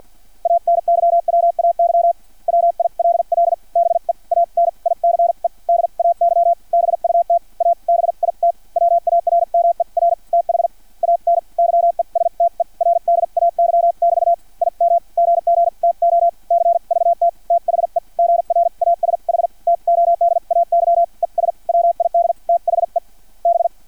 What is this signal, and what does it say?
Morse code, heard as a keyed tone, about 50 wpm, spaced as for 41 wpm, 680 Hz. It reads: ANYWAY WILL BE AN ICE DAY BUT A BIT WARMER THAN YESTERDAY<BT> IM GG TO CVT THE GRASH TODAY ES WID THE B
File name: archived / CRSnet